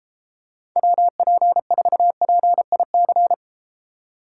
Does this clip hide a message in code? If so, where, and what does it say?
Yes: Morse code from 0.8 to 3.3 s, heard as a keyed tone, about 33 wpm, 700 Hz. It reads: WP4PIC